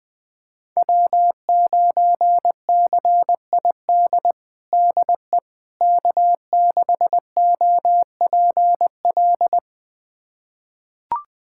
Morse code, heard as a keyed tone, about 20 wpm, 700 Hz. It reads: W9CID DE K6OPL
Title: USA Callsign DE Callsign - 5 Characters - 20wpm